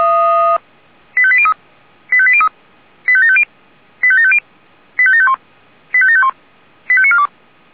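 Five-tone selective call (ZVEI), heard as quick runs of five tones, tones of about 70 ms. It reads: 85803 85803 85750 85750 85751 85751 85942
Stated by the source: ZVEI five-tone call received on SDR.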